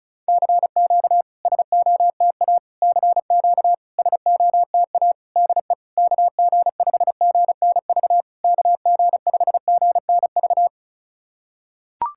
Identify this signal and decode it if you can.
Morse code, heard as a keyed tone, about 36 wpm, 690 Hz. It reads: CQ SOTA CQ SOTA DE KG5GNV KG5GNV
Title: SOTA QSO Practice 35wpm